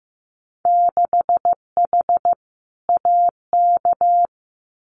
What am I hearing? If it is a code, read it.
Morse code, heard as a keyed tone, about 15 wpm, 700 Hz. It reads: 6H AK
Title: Winter Field Day Exchanges Encore 15wpm